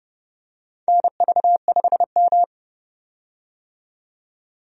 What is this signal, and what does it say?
Morse code, heard as a keyed tone, about 30 wpm, 700 Hz. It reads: NV5M
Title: USA Call Signs - 4 Characters - 30wpm